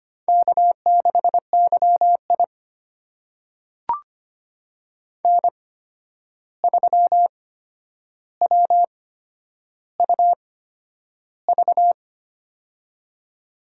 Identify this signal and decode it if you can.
Morse code, heard as a keyed tone, about 25 wpm, spaced as for 10 wpm, 700 Hz. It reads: K6YI N3WUV
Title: USA Call Signs - 8x Character Spacing - 25wpm